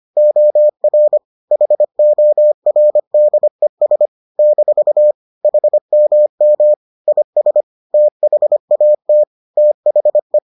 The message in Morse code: OR HORDES - HMM IS THAT THE

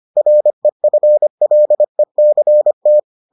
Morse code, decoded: REFLECT

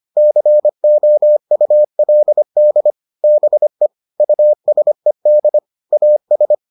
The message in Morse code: COULD BE USED AS

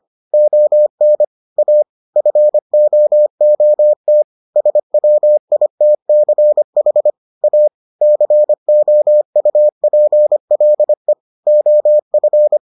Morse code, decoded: ON A FOOT SWITCH A COUPLE OF

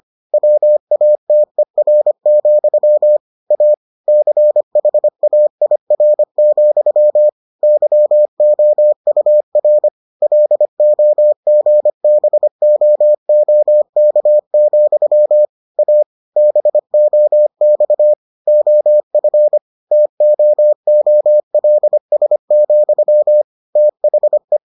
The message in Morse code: WATER, A CHAIR, YOUR LOGBOOK, A BOX OF TOOLS, THE

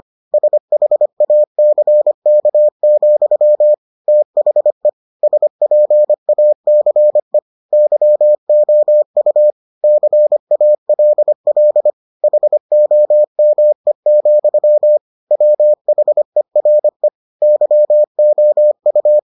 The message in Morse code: SHACK, THE SPACE YOU CALL HOME, WHERE YOU

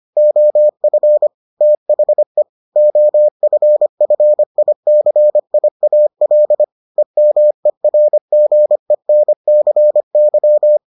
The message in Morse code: OF THE OFFICIAL EMERGENCY

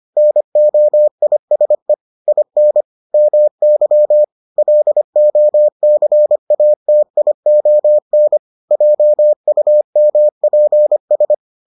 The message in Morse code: NOISE IN MY LOCATION JUMPS

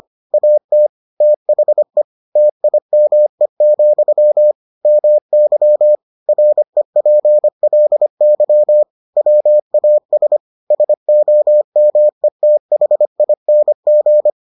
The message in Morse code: AT THE TIME, MY REPLY WAS SOMETHING

